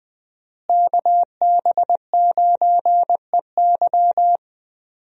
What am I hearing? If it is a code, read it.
Morse code, heard as a keyed tone, about 20 wpm, 700 Hz. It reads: KB9EY
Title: USA Call Signs - 5 Characters - Repeated 2x - 20wpm